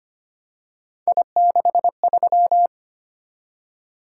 Morse code, heard as a keyed tone, about 25 wpm, 700 Hz. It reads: I63
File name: Three Letter-Number Combinations - 25wpm